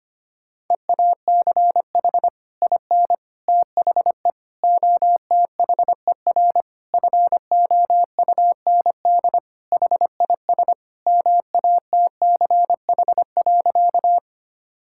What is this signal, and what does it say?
Morse code, heard as a keyed tone, about 25 wpm, 700 Hz. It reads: EACH IN THE OTHER FOUND HIS MATCH.
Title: Sentences from Top 1600 Words 25wpm